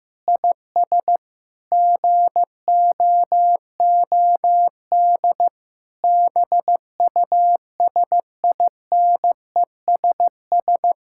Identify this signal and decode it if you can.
Morse code, heard as a keyed tone, about 15 wpm, 700 Hz. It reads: IS GOOD BUSINESS
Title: Sentences from Top 1000 Words - Review Part 5 - 15wpm